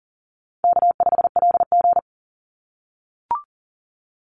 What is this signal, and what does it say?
Morse code, heard as a keyed tone, about 40 wpm, 700 Hz. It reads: K5LG
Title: Easy Call Signs - 40wpm